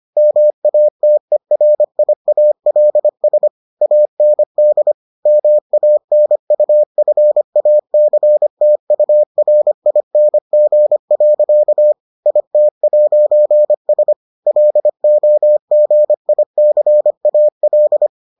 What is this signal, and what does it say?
Morse code, heard as a keyed tone, about 25 wpm, 600 Hz. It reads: MATERIALS AND MANUFACTURING. IT'S LOGICAL